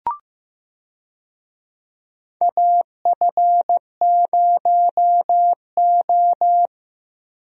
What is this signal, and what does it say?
Morse code, heard as a keyed tone, about 15 wpm, 700 Hz. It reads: AF0O
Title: USA Call Signs - 4 Characters - Repeated 3x - 15wpm